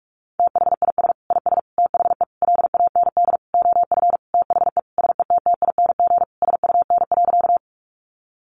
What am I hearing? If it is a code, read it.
Morse code, heard as a keyed tone, about 44 wpm, 700 Hz. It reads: THIS IS THE LAND OF THE SETTING SUN.
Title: Sentences from Top 1300 Words 45wpm